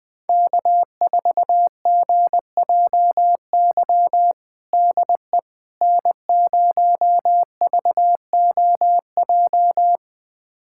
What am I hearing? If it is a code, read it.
Morse code, heard as a keyed tone, about 21 wpm, 700 Hz. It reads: K4GJY DE N0VOJ
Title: USA Callsign DE Callsign - 5 Characters - 20wpm